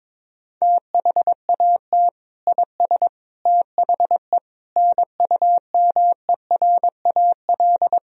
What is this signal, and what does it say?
Morse code, heard as a keyed tone, about 22 wpm, 700 Hz. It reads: THAT IS THE NUMERAL
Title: Sentences from Top 1000 Words - Review Part 2 - 22wpm